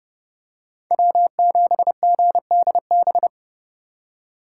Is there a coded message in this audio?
Yes: Morse code from 0.9 to 3.3 s, heard as a keyed tone, about 30 wpm, 700 Hz. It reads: W7GDB